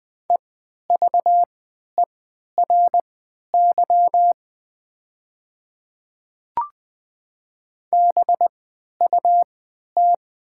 Morse code, heard as a keyed tone, about 20 wpm, spaced as for 11 wpm, 700 Hz. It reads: EVERY BUT
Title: 3x Character Spacing - Top 300 Words - 20wpm